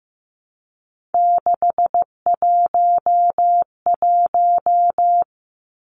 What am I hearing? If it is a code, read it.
Morse code, heard as a keyed tone, about 15 wpm, 700 Hz. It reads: 611
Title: Numbers - 3-Digit 15wpm